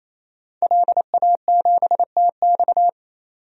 Morse code, heard as a keyed tone, about 28 wpm, 700 Hz. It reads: LA7TX
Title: DXCC-100 Call Signs - 5 Characters - 28wpm